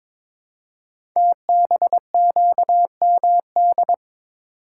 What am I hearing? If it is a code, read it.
Morse code, heard as a keyed tone, about 21 wpm, 700 Hz. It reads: TBQMD
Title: Code Groups - 5 Letters - 22wpm